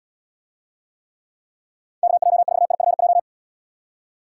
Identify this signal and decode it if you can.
Morse code, heard as a keyed tone, about 76 wpm, 700 Hz. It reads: KJ4IFJ